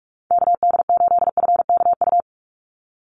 Morse code, heard as a keyed tone, about 44 wpm, 700 Hz. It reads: KD8FKU
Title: USA Call Signs 45wpm